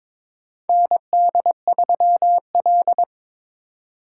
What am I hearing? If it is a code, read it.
Morse code, heard as a keyed tone, about 22 wpm, 700 Hz. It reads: ND3L
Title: K1USN SST QSO Practice - 22wpm